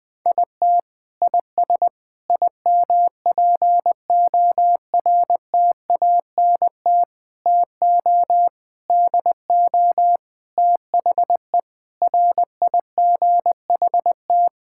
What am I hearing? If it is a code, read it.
Morse code, heard as a keyed tone, about 20 wpm, 700 Hz. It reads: IT IS IMPORTANT TO DO THE RIGHT
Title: Sentences with 5-Word Phrases - 20wpm